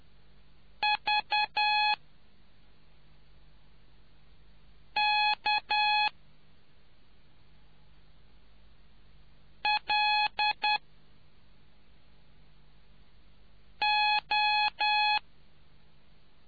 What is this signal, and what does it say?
Morse code, heard as a keyed tone, about 10 wpm, 840 Hz. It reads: V K L O